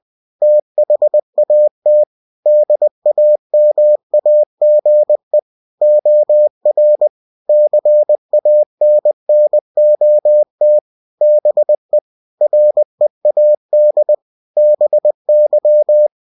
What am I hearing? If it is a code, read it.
Morse code, heard as a keyed tone, about 20 wpm, 600 Hz. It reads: THAT DAMAGE OR CANNOT BE READ BY